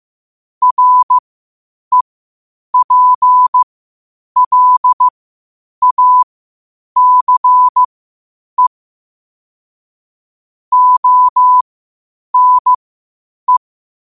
Morse code, heard as a keyed tone, about 15 wpm, spaced as for 8 wpm, 1000 Hz. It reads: REPLACE ONE
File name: qotdfiles / 8wpm